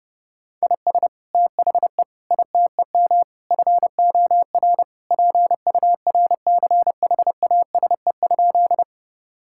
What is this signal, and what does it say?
Morse code, heard as a keyed tone, about 30 wpm, 700 Hz. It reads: IS THE ITEM FOR PURCHASE?
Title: Sentences from Top 1000 Words - Review Part 6 - 30wpm